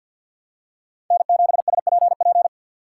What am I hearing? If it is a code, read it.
Morse code, heard as a keyed tone, about 49 wpm, 700 Hz. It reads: N7SPP
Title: USA Call Signs - 5 Characters - 4x Character Spacing - 50wpm